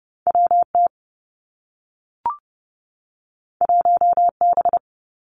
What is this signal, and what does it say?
Morse code, heard as a keyed tone, about 30 wpm, 700 Hz. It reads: W T 1 B
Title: Two Challenging Letter-Number Combinations 30wpm